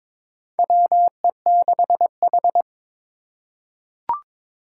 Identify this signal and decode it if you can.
Morse code, heard as a keyed tone, about 22 wpm, 700 Hz. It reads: WE6H
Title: USA Call Signs - 4 Characters - 3x Character Spacing - 22wpm